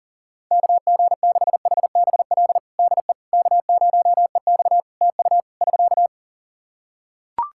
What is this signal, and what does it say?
Morse code, heard as a keyed tone, about 40 wpm, 700 Hz. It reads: KG6HBL DE K0EX TU <SK>
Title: K1USN SST QSO Practice 40wpm